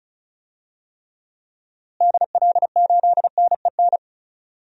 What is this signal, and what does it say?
Morse code, heard as a keyed tone, about 36 wpm, 700 Hz. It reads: DL8NEN
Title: DXCC-100 Call Signs - 6 Characters - Repeated 3x - 35wpm